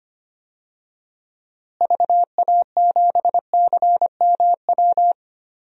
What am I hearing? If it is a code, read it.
Morse code, heard as a keyed tone, about 25 wpm, 700 Hz. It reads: VA7CMW
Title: DXCC-100 Call Signs - 6 Characters - Repeated 3x - 25wpm